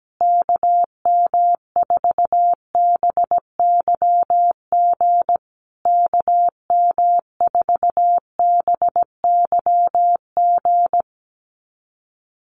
Morse code, heard as a keyed tone, about 17 wpm, 700 Hz. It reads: KM4BYG KM4BYG
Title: USA Call Signs Repeated 3x 17wpm